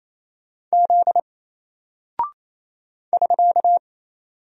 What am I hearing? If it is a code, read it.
Morse code, heard as a keyed tone, about 28 wpm, 700 Hz. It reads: Z <SK>
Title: Single Character (Letter:Number:Punctuation:Prosign) Alt 28wpm